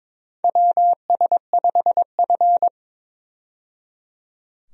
Morse code, heard as a keyed tone, about 22 wpm, 700 Hz. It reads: WS5F
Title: USA Call Signs - 4 Characters - 22wpm